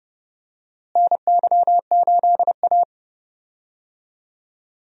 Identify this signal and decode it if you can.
Morse code, heard as a keyed tone, about 30 wpm, 700 Hz. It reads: NY8A
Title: USA Call Signs - 4 Characters - Repeated 3x - 30wpm